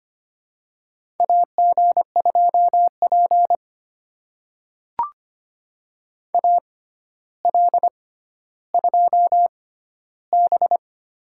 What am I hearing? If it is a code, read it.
Morse code, heard as a keyed tone, about 25 wpm, spaced as for 13 wpm, 700 Hz. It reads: AG2P AL2B
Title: USA Call Signs - 4 Characters - 6x Character Spacing - 25wpm